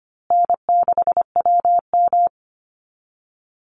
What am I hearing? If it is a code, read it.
Morse code, heard as a keyed tone, about 25 wpm, 700 Hz. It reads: N6WM
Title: Call Signs Repeated 2x - 25wpm